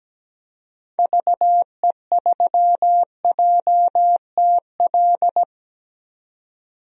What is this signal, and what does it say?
Morse code, heard as a keyed tone, about 17 wpm, 700 Hz. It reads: VE3JTL
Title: DXCC-100 Call Signs - 6 Characters - 17wpm